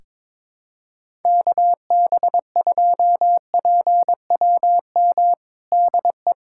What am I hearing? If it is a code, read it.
Morse code, heard as a keyed tone, about 22 wpm, 700 Hz. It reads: KB2PWM DE
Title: USA Callsign DE Callsign - 6 Characters - 22wpm